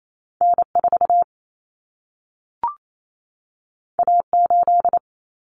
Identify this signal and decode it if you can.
Morse code, heard as a keyed tone, about 28 wpm, 700 Hz. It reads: N4 A8
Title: Common Cut Numbers - 2-Digit 28wpm